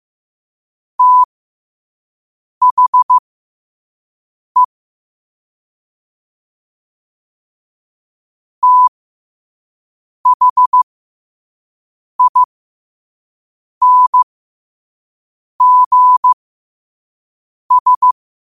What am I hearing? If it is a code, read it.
Morse code, heard as a keyed tone, about 15 wpm, spaced as for 5 wpm, 1000 Hz. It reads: THE THINGS
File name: qotdfiles / 5wpm